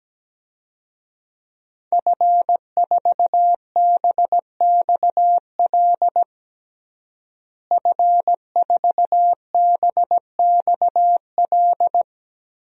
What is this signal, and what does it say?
Morse code, heard as a keyed tone, about 17 wpm, 700 Hz. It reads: F4BXL F4BXL
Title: DXCC-100 Call Signs - 5 Characters - Repeated 3x - 17wpm